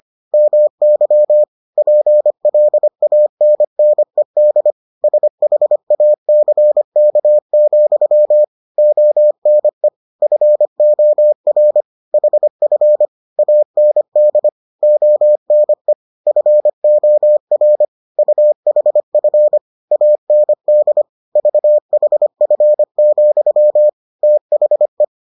Morse code, heard as a keyed tone, about 25 wpm, 600 Hz. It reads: MY PLANNED SHACK, ONE FOR HF AND ONE FOR UHF AND VHF, THE